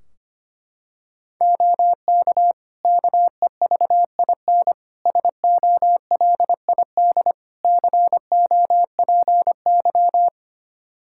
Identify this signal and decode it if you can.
Morse code, heard as a keyed tone, about 25 wpm, 700 Hz. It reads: OK KEVIN SOLID COPY
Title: Practice QSO - Exchange 3 - More Info - 25wpm